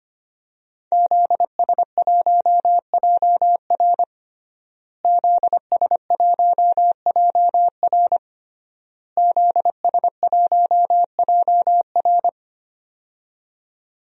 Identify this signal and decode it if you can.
Morse code, heard as a keyed tone, about 25 wpm, 700 Hz. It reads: ZS1JR ZS1JR ZS1JR
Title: DXCC-100 Call Signs - 5 Characters - Repeated 3x - 25wpm